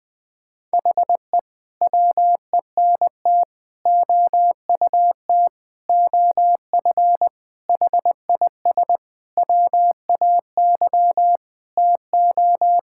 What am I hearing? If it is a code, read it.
Morse code, heard as a keyed tone, about 20 wpm, 700 Hz. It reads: HE WENT OUT OF HIS WAY TO
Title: Sentences with 6-Word Phrases - Spoken Beginning - 20wpm